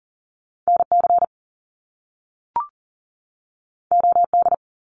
Morse code, heard as a keyed tone, about 40 wpm, 700 Hz. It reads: NC OD
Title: Top Two Letter Combinations in Words 40wpm